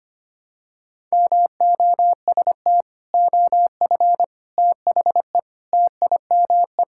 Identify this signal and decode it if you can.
Morse code, heard as a keyed tone, about 25 wpm, 700 Hz. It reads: MOST OF THE TIME
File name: practice / 25wpm / Sentences with 4-Word Phrases - 25wpm